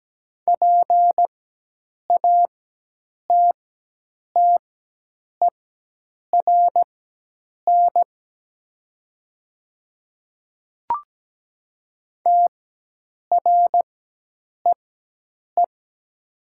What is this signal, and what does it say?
Morse code, heard as a keyed tone, about 17 wpm, spaced as for 8 wpm, 700 Hz. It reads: PATTERN TREE
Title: 4x Character Spacing - Top 500 Words - 17wpm